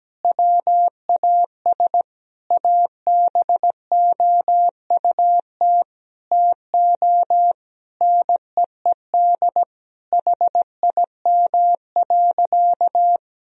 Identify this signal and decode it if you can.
Morse code, heard as a keyed tone, about 17 wpm, 700 Hz. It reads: WAS ABOUT TO NEED HIM.
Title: Sentences from Top 1000 Words - Review Part 1 - 17wpm